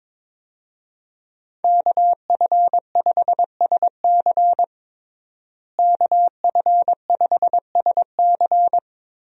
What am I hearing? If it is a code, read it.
Morse code, heard as a keyed tone, about 22 wpm, 700 Hz. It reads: KF5SC KF5SC
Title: USA Call Signs - 5 Characters - Repeated 2x - 22wpm